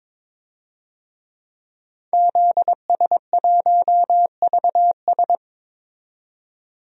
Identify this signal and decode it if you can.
Morse code, heard as a keyed tone, about 22 wpm, 700 Hz. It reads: ZS1VS
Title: DXCC-100 Call Signs - 5 Characters - 22wpm